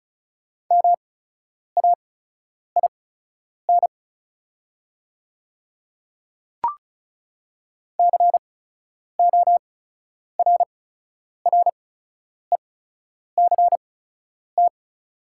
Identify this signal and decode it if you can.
Morse code, heard as a keyed tone, about 36 wpm, spaced as for 10 wpm, 700 Hz. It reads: MAIN CORRECT
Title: 8x Character Spacing - Top 500 Words - 35wpm